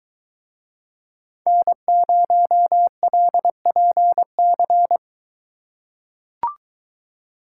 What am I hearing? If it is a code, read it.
Morse code, heard as a keyed tone, about 22 wpm, 700 Hz. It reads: N0LPC